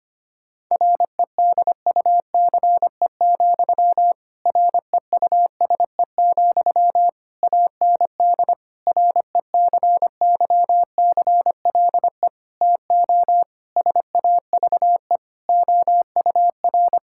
Morse code, heard as a keyed tone, about 25 wpm, 700 Hz. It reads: REDUCE, REUSE, AND RECYCLE TO SAVE OUR
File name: practice / 25wpm / Difficult Sentences with Hints - 25wpm